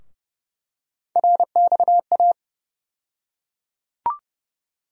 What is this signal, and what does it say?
Morse code, heard as a keyed tone, about 30 wpm, 700 Hz. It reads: RXA
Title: USA Call Sign Postfixes - 3 Letters 30wpm